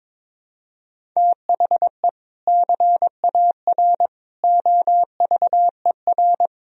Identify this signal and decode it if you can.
Morse code, heard as a keyed tone, about 22 wpm, 700 Hz. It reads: THE CAR OVER
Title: Sentences from Top 1000 Words - Review Part 1 - 22wpm